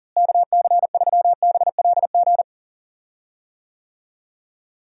Morse code, heard as a keyed tone, about 40 wpm, 700 Hz. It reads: KC3BLG
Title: USA Call Signs - 6 Characters - Repeated 3x - 40wpm